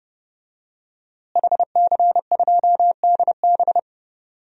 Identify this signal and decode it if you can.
Morse code, heard as a keyed tone, about 32 wpm, 700 Hz. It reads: HC2DB
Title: DXCC-100 Call Signs - 5 Characters - Repeated 2x - 30wpm